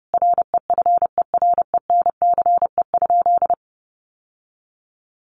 Morse code, heard as a keyed tone, about 30 wpm, 700 Hz. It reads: REFERENCE?
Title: Sentences from Top 1200 Words 30wpm